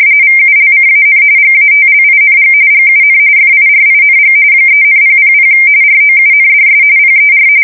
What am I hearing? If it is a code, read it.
RTTY, about 45.45 baud. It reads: VULCAN GATLING GUN FIRING 3,000 ROUNDS PER
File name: fsk170 / USS-New-Jersey